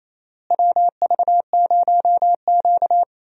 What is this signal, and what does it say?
Morse code, heard as a keyed tone, about 28 wpm, 700 Hz. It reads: WV0Q
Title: USA Call Signs - 4 Characters - Repeated 3x - 28wpm